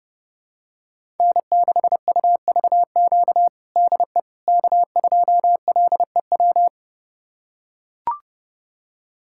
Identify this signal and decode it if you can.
Morse code, heard as a keyed tone, about 30 wpm, 700 Hz. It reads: N6UVQ DE K2LEW
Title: USA Callsign DE Callsign - 5 Characters - 30wpm